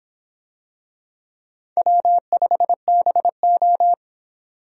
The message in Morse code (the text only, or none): W5BO